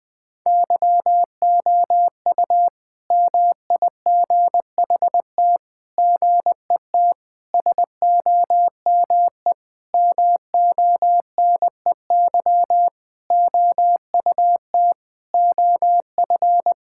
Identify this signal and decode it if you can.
Morse code, heard as a keyed tone, about 20 wpm, 700 Hz. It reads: YOU MIGHT GET SOME MONEY OUT OF
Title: Sentences from Top 1000 Words - Review Part 5 - 20wpm